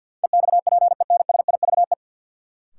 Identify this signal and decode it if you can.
Morse code, heard as a keyed tone, about 49 wpm, 700 Hz. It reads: EXPENSIVE
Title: Sentences from Top 1500 Words - New Word Drill Alt - 50wpm